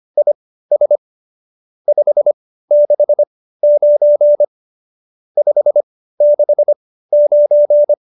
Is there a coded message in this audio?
Yes: Morse code from 0.2 to 7.9 s, heard as a keyed tone, about 25 wpm, spaced as for 15 wpm, 590 Hz. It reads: IS 569 569